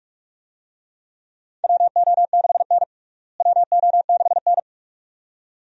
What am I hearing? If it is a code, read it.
Morse code, heard as a keyed tone, about 46 wpm, 700 Hz. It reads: WO6N WO6N
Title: USA Call Signs - 4 Characters - Repeated 2x - 45wpm